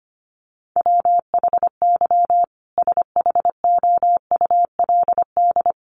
Morse code, heard as a keyed tone, about 25 wpm, 700 Hz. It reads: WHY SHOULD
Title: Sentences from Top 300 Words 25wpm